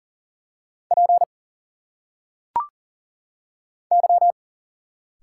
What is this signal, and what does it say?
Morse code, heard as a keyed tone, about 40 wpm, 700 Hz. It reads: P Y
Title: Single Letters - Phonetic Pronunciation 40wpm